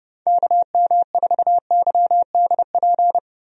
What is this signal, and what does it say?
Morse code, heard as a keyed tone, about 32 wpm, 700 Hz. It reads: KM4YDP
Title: USA Call Signs - 6 Characters - 30wpm